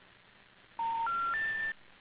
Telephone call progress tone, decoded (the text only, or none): special information tone